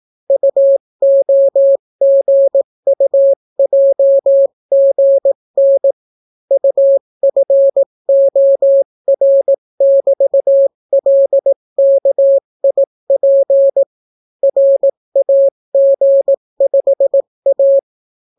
Morse code, heard as a keyed tone, about 18 wpm, 550 Hz. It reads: UOGUJGN UFOR<BT>LKIP RAG5A